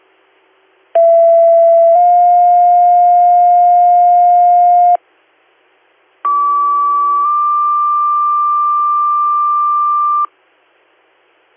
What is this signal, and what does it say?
Two-tone page, heard as a one-second tone, then a three-second tone, about 670 then 710 Hz, about 1150 then 1180 Hz. It is heard from 0.9 to 10.3 s.